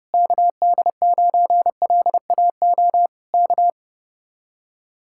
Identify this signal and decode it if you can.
Morse code, heard as a keyed tone, about 31 wpm, 700 Hz. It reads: KD9LAO K